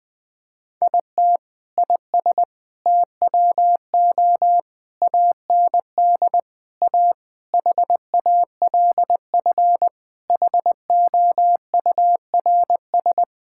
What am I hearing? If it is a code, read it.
Morse code, heard as a keyed tone, about 20 wpm, 700 Hz. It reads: IT IS TWO AND A HALF HOURS
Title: Sentences with 4-Word Phrases - 20wpm